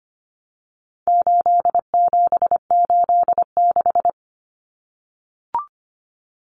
Morse code, heard as a keyed tone, about 25 wpm, 700 Hz. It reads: 8786
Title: Numbers - 4-Digit 25wpm